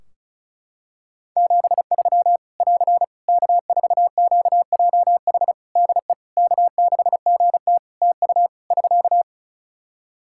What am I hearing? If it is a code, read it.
Morse code, heard as a keyed tone, about 36 wpm, 700 Hz. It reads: 73 <AR> K4QJH DE K6GT TU <SK>
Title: Practice QSO - Exchange 4 - Ending - 35wpm